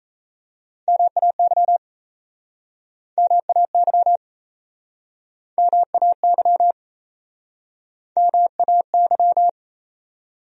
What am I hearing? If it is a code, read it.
Morse code, heard as a keyed tone, about 40 wpm, 700 Hz. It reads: MAY MAY MAY MAY